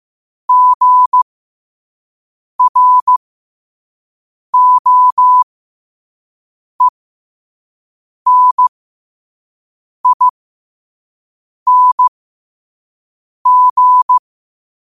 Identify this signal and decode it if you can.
Morse code, heard as a keyed tone, about 15 wpm, spaced as for 5 wpm, 1000 Hz. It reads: GROENING
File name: qotdfiles / 5wpm